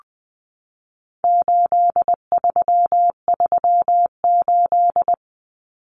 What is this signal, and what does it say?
Morse code, heard as a keyed tone, about 20 wpm, 700 Hz. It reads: 8338
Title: Numbers - 4-Digit 20wpm